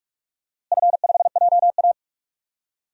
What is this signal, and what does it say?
Morse code, heard as a keyed tone, about 43 wpm, 700 Hz. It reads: F5JA
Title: DXCC-100 Call Signs - 4 Characters - Repeated 3x - 45wpm